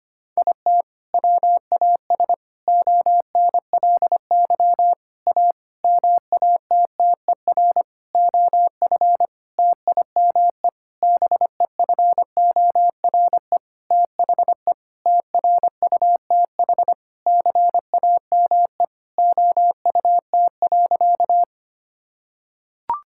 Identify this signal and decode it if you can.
Morse code, heard as a keyed tone, about 25 wpm, 700 Hz. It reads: IT WAS ONLY A MATTER OF TIME BEFORE THE TRUTH CAME OUT.
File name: practice / 25wpm / Sentences with 9-Word Phrases - 25wpm